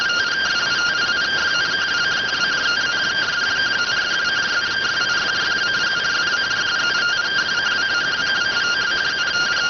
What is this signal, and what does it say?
Two-tone teleprinter signal, about 170 Hz shift.